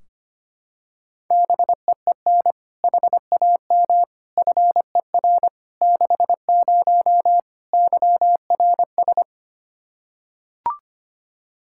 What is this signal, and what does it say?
Morse code, heard as a keyed tone, about 25 wpm, 700 Hz. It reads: BEEN HAM FER 60 YRS
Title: Practice QSO - Exchange 3 - More Info - 25wpm